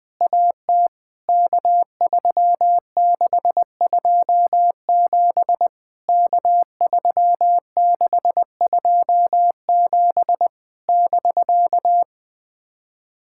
Morse code, heard as a keyed tone, about 20 wpm, 700 Hz. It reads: AT K3627 K3627 <BK>
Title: POTA QSO Practice - 20wpm